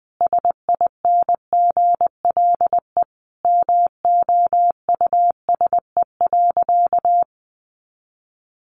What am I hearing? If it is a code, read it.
Morse code, heard as a keyed tone, about 20 wpm, 700 Hz. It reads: SINGLE MOUSE.
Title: A Sound Of Thunder 20wpm